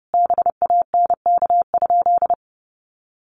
Morse code, heard as a keyed tone, about 30 wpm, 700 Hz. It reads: BANK?
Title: Sentences from Top 1300 Words 30wpm